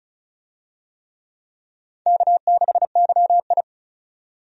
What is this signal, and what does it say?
Morse code, heard as a keyed tone, about 36 wpm, 700 Hz. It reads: K6YI
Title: USA Call Signs - 4 Characters - 35wpm